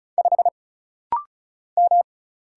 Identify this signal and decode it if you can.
Morse code, heard as a keyed tone, about 35 wpm, 700 Hz. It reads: 5 M
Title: Single Letter-Number Alt - Rapid-Fire 35wpm